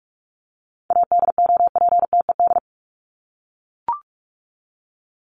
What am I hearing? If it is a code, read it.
Morse code, heard as a keyed tone, about 44 wpm, 700 Hz. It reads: ADOPTED
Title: ED Suffix - Top 100 Words with Base Words 45wpm